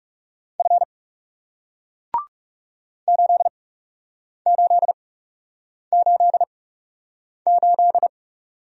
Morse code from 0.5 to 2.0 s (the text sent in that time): F